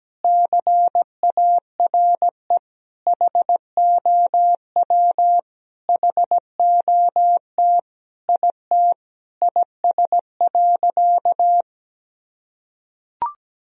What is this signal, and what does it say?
Morse code, heard as a keyed tone, about 17 wpm, 700 Hz. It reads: CARE HOW HOT IT IS.
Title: Sentences from Top 1000 Words - Review Part 3 - 17wpm